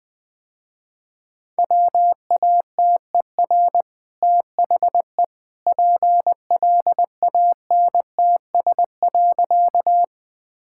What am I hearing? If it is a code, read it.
Morse code, heard as a keyed tone, about 20 wpm, 700 Hz. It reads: WATER THE PLANTS.
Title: Easy Sentences with Hints - 20wpm